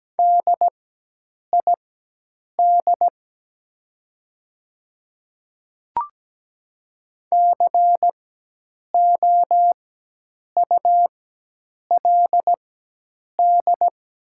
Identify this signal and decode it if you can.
Morse code, heard as a keyed tone, about 17 wpm, spaced as for 8 wpm, 700 Hz. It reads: DID COULD
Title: 4x Character Spacing - Top 100 Words - 17wpm